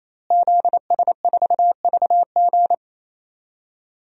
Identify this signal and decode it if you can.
Morse code, heard as a keyed tone, about 27 wpm, 700 Hz. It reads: ZS4VG
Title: DXCC-100 Call Signs - 5 Characters - Repeated 3x - 28wpm